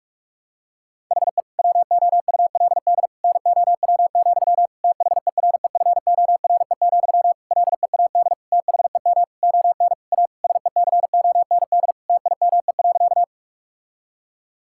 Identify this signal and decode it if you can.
Morse code, heard as a keyed tone, about 46 wpm, 700 Hz. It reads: HE WOULD NOW, THEREFORE, LEAD THEM ON A SECOND TIME.
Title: Sentences from Top 1000 Words - Review Part 5 - 45wpm